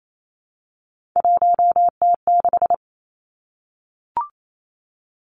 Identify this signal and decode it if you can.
Morse code, heard as a keyed tone, about 28 wpm, 700 Hz. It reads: 1T6
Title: Three Challenging Letter-Number Combinations 28wpm